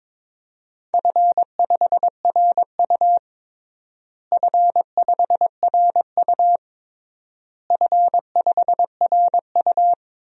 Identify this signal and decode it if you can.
Morse code, heard as a keyed tone, about 22 wpm, 700 Hz. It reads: F5RU F5RU F5RU
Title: DXCC-100 Call Signs - 4 Characters - Repeated 3x - 22wpm